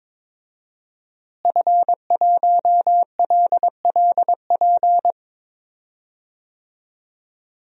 Morse code, heard as a keyed tone, about 22 wpm, 700 Hz. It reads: F1LLP
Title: DXCC-100 Call Signs - 5 Characters - Repeated 2x - 22wpm